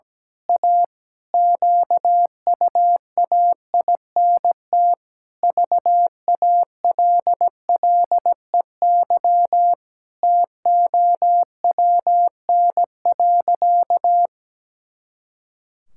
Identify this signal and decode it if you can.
Morse code, heard as a keyed tone, about 17 wpm, 700 Hz. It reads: A QUAINT VALLEY TOWN.
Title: Pangram Challenge v2 - 17wpm